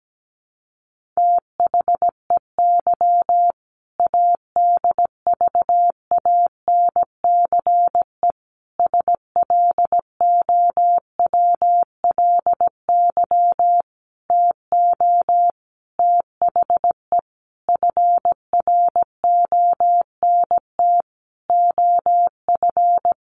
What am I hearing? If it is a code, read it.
Morse code, heard as a keyed tone, about 17 wpm, 700 Hz. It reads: THEY ADVANCE SLOWLY TO THE FRONT OF
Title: Sentences from Top 1400 Words 17wpm